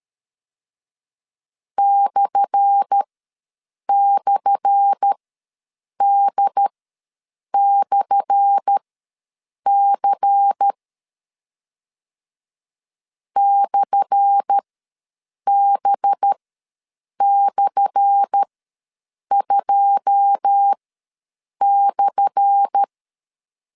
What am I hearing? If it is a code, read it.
Morse code, heard as a keyed tone, about 13 wpm, spaced as for 7 wpm, 800 Hz. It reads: //D/C /B/2/
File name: LEZIONI / 10 SIGNS / 10A SLASH ws 12wpm 3443